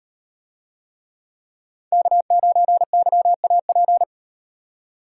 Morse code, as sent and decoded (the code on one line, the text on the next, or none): -.- ----. -.-- .- .--.
K9YAP